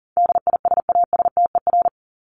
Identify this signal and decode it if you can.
Morse code, heard as a keyed tone, about 40 wpm, 700 Hz. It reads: DISASTER
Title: Top 1501-1750 Words 40wpm